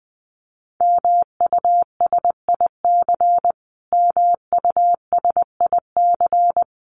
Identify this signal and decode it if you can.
Morse code, heard as a keyed tone, about 20 wpm, 700 Hz. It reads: MUSIC MUSIC
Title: Top 500 Words Repeated 20wpm